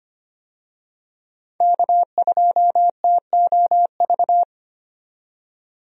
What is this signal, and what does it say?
Morse code, heard as a keyed tone, about 25 wpm, 700 Hz. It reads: K2TOV
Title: USA Call Signs - 5 Characters - 25wpm